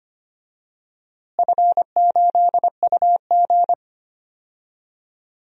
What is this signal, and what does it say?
Morse code, heard as a keyed tone, about 25 wpm, 700 Hz. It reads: F8UG
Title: DXCC-100 Call Signs - 4 Characters - Repeated 2x - 25wpm